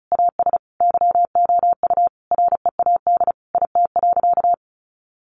Morse code, heard as a keyed tone, about 34 wpm, 700 Hz. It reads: AS YOU READ IT.
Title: Sentences from Top 1300 Words 35wpm